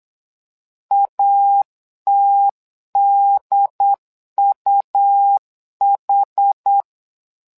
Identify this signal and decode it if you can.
Morse code, heard as a keyed tone, about 8 wpm, 800 Hz. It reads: ATDUH
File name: LEZIONI / 2 UHDN / 2 UHD 8wpm 2258